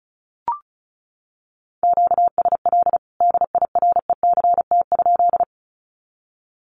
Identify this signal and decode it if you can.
Morse code, heard as a keyed tone, about 34 wpm, 700 Hz. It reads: QSL DIRECT?
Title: Common QSO Phrases 35wpm